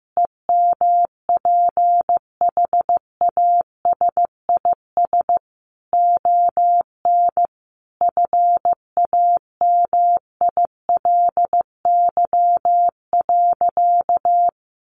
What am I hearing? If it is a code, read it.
Morse code, heard as a keyed tone, about 15 wpm, 700 Hz. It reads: EMPHASIS ON FAMILY.
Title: Sentences from Top 1400 Words 15wpm